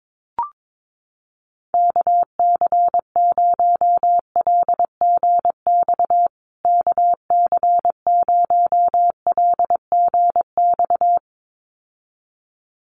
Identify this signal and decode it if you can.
Morse code, heard as a keyed tone, about 22 wpm, 700 Hz. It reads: KC0LGX KC0LGX
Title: USA Call Signs Repeated 2x 22wpm